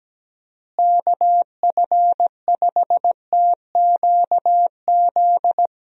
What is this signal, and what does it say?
Morse code, heard as a keyed tone, about 17 wpm, 700 Hz. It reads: KF5TQZ
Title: USA Call Signs - 6 Characters - Repeated 2x - 17wpm